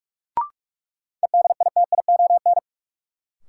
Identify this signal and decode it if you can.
Morse code, heard as a keyed tone, about 46 wpm, 700 Hz. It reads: EDITION
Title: Sets of 5 Words - New Word Drill - 45wpm